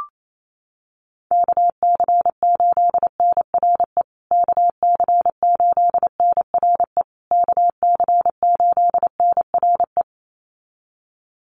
Morse code, heard as a keyed tone, about 28 wpm, 700 Hz. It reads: KC8NRE KC8NRE KC8NRE
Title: USA Call Signs Repeated 3x 28wpm